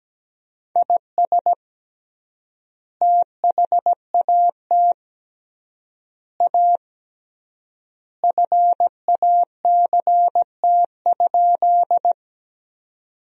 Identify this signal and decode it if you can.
Morse code, heard as a keyed tone, about 17 wpm, 700 Hz. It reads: IS THAT A FACT?
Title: 3x Word-Spacing - Sentences from Top 500 Words - 17wpm